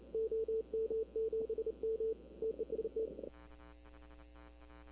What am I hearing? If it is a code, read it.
Morse code, heard as a keyed tone, about 32 wpm, 440 Hz. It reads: OM7M TEST
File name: CW / 2005-k5tr arrl cw dx -20:00